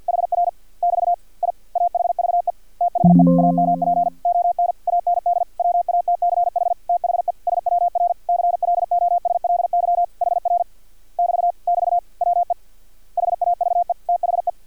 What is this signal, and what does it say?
Morse code, heard as a keyed tone, about 50 wpm, 700 Hz. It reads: HR <BT> I AHVE THE AMP ON AND WATCH THE SWR CLOSLY HR <BT> <BT> WE HAVE THE
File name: archived / CRSnet